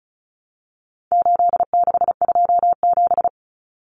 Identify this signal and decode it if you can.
Morse code, heard as a keyed tone, about 34 wpm, 700 Hz. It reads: 8627
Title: Numbers - 4-Digit 35wpm